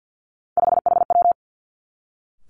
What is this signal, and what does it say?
Morse code, heard as a keyed tone, about 50 wpm, 700 Hz. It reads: 5HW
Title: Three Challenging Letter-Number Combinations 50wpm